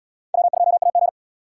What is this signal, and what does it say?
Morse code, heard as a keyed tone, about 70 wpm, 700 Hz. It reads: W2AG